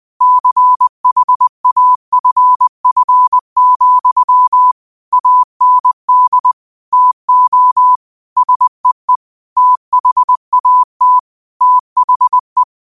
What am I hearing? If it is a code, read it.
Morse code, heard as a keyed tone, about 20 wpm, 1000 Hz. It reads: CHAFF, AND TO SEE THAT THE
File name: qotdfiles / 20wpm